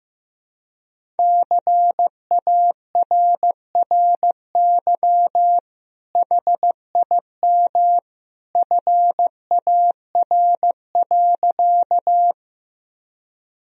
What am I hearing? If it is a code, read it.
Morse code, heard as a keyed tone, about 15 wpm, 700 Hz. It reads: CARRY HIM FAR.
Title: Sentences from Top 1000 Words - Review Part 1 - 15wpm